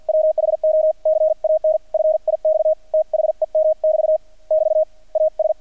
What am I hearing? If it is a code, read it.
Morse code, heard as a keyed tone, about 50 wpm, spaced as for 44 wpm, 630 Hz. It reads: WHO CAN FIX THEM <BT> <BT> AS
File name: archived / CRSnet